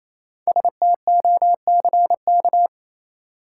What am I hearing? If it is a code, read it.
Morse code, heard as a keyed tone, about 28 wpm, 700 Hz. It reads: STOCK
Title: Sets of 5 Words - New Word Drill - 28wpm